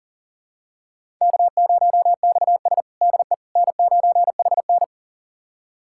Morse code, heard as a keyed tone, about 40 wpm, 700 Hz. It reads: K0XS DE N9HN
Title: USA Callsign DE Callsign - 4 Characters - 40wpm